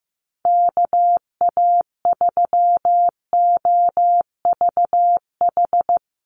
Morse code, heard as a keyed tone, about 15 wpm, 700 Hz. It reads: KA3OVH
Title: USA Call Signs Repeated 2x 15wpm